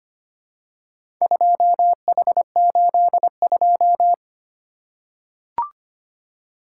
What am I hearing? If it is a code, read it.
Morse code, heard as a keyed tone, about 25 wpm, 700 Hz. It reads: 2H82
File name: practice / 25wpm / Four Letter-Number Combinations - 25wpm